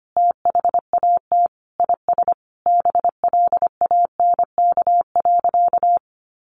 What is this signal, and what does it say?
Morse code, heard as a keyed tone, about 25 wpm, 700 Hz. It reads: THAT IS BLANK.
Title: Sentences from Top 1500 Words 25wpm